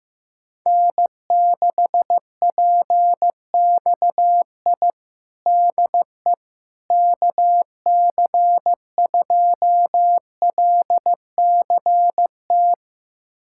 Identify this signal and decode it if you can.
Morse code, heard as a keyed tone, about 15 wpm, 700 Hz. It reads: N6PXI DE KC2LCT
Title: USA Callsign DE Callsign 15wpm